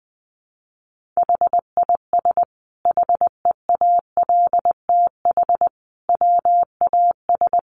Morse code, heard as a keyed tone, about 19 wpm, 700 Hz. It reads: HIS HEALTH WAS
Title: Sentences from Top 1400 Words 20wpm